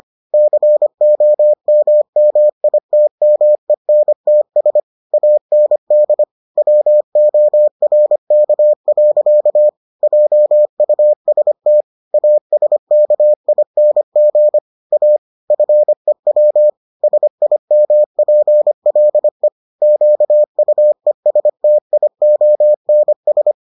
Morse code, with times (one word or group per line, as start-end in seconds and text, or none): 0.3-4.8 COMMITMENTS
5.1-6.2 AND
6.6-9.7 WORK.
10.0-11.8 JUST
12.1-14.6 ASKING
14.9-15.2 A
15.5-16.7 FEW
17.0-19.5 SIMPLE
19.8-23.5 QUESTIONS